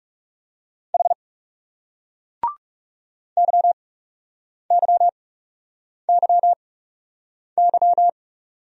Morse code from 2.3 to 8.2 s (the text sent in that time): YYYY